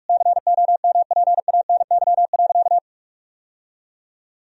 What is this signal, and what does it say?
Morse code, heard as a keyed tone, about 46 wpm, 700 Hz. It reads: COMPANY.
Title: Sentences from Top 1000 Words - Review Part 1 - 45wpm